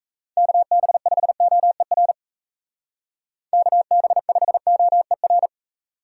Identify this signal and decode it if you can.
Morse code, heard as a keyed tone, about 33 wpm, 700 Hz. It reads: KB5OER KB5OER
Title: Speed Racing - USA 2x3 Call Signs 28wpm